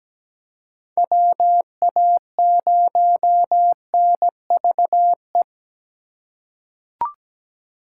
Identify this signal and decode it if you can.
Morse code, heard as a keyed tone, about 17 wpm, 700 Hz. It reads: WA0NVE
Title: USA Call Signs - 6 Characters - 2x Character Spacing - 17wpm